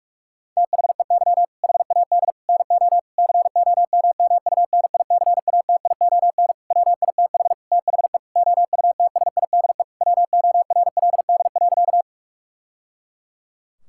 Morse code, heard as a keyed tone, about 46 wpm, 700 Hz. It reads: THEY HAD NO COMMUNICATION WITH THE OUTSIDE WORLD.
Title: Sentences from Top 1000 Words - Review Part 6 - 45wpm